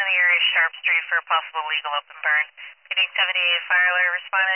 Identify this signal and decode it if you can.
Emergency Alert System (SAME) tones, heard as harsh buzzing bursts, with speech. It is heard from 0.5 to 4.3 s.